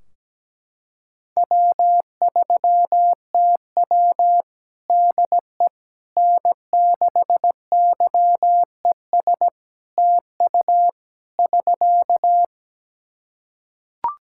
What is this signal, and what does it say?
Morse code, heard as a keyed tone, about 17 wpm, 700 Hz. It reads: W3TW DE N6YES TU <SK>
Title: Practice QSO - Exchange 4 - Ending - 17wpm